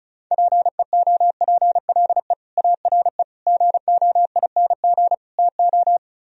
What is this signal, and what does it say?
Morse code, heard as a keyed tone, about 45 wpm, 700 Hz. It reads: PEOPLE ARE GOING TO